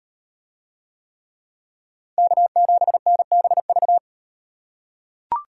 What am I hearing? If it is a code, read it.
Morse code, heard as a keyed tone, about 38 wpm, 700 Hz. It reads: K7NBV